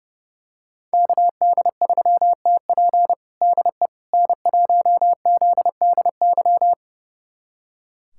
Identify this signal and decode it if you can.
Morse code, heard as a keyed tone, about 30 wpm, 700 Hz. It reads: KD3TP DE N1ZDY
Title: USA Callsign DE Callsign - 5 Characters - 30wpm